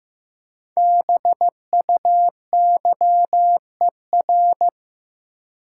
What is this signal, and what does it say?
Morse code, heard as a keyed tone, about 16 wpm, 700 Hz. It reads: BUYER
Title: Five Letter Words - 15wpm